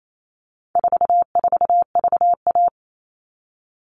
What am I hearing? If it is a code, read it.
Morse code, heard as a keyed tone, about 28 wpm, 700 Hz. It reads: 44VA
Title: All Cut Numbers - 4-Digit 28wpm